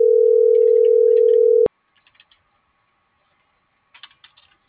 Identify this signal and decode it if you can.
Ringback tone (North America), 440 + 480 Hz.